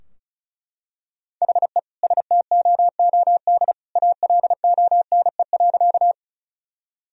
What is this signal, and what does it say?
Morse code, heard as a keyed tone, about 36 wpm, 700 Hz. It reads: HE STOOD ALONE.
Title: Sentences from Top 1000 Words - Review Part 2 - 35wpm